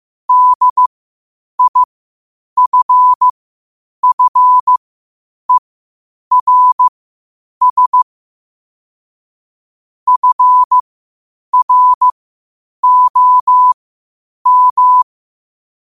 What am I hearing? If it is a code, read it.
Morse code, heard as a keyed tone, about 15 wpm, spaced as for 8 wpm, 1000 Hz. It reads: DIFFERS FROM
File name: qotdfiles / 8wpm